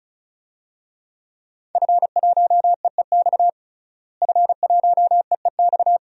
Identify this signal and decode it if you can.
Morse code, heard as a keyed tone, about 34 wpm, 700 Hz. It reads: F1EEX F1EEX
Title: DXCC-100 Call Signs - 5 Characters - Repeated 2x - 35wpm